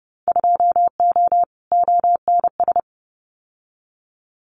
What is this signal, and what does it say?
Morse code, heard as a keyed tone, about 30 wpm, 700 Hz. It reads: 2O ONS
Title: Winter Field Day Exchanges Encore 30wpm